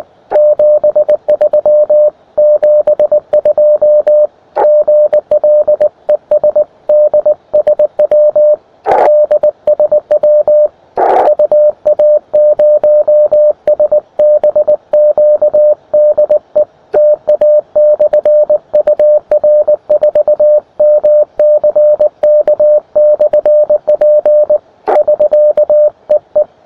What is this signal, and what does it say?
Morse code, heard as a keyed tone, about 19 wpm, 600 Hz. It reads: 73 72 GL ES DSW DSW UA0SBQ DE TA/UR4MCK/P <SK> EE
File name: attachments / ua0sbq 20m